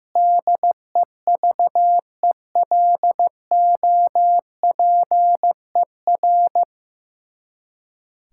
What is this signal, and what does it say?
Morse code, heard as a keyed tone, about 15 wpm, 700 Hz. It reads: DEVELOPER
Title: Practice QSO - Exchange 3 - More Info - 15wpm